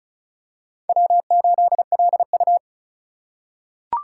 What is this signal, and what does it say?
Morse code, heard as a keyed tone, about 36 wpm, 700 Hz. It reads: W8LU
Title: USA Call Signs - 4 Characters - 4x Character Spacing - 35wpm